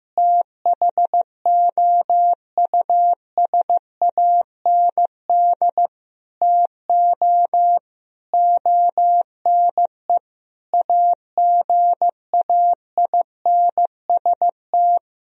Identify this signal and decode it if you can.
Morse code, heard as a keyed tone, about 15 wpm, 700 Hz. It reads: THOUSAND TO ONE AGAINST
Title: Sentences from Top 1000 Words - Review Part 3 - 15wpm